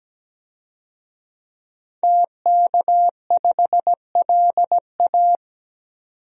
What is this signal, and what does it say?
Morse code, heard as a keyed tone, about 17 wpm, 700 Hz. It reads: TK5LA
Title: DXCC-100 Call Signs - 5 Characters - 17wpm